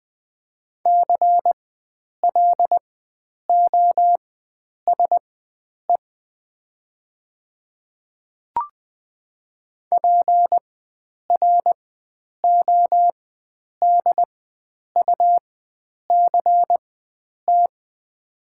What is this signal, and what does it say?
Morse code, heard as a keyed tone, about 20 wpm, spaced as for 9 wpm, 700 Hz. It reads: CLOSE PRODUCT
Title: 4x Character Spacing - Top 500 Words - 20wpm